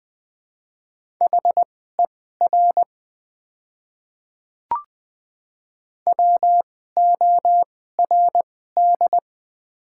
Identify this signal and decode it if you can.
Morse code, heard as a keyed tone, about 20 wpm, spaced as for 14 wpm, 700 Hz. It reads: HER WORD
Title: 2x Character Spacing - Top 300 Words - 20wpm